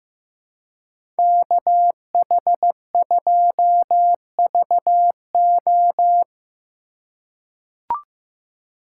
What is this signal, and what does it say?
Morse code, heard as a keyed tone, about 15 wpm, 700 Hz. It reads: KH2VO
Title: USA Call Signs - 4x Character Spacing - 15wpm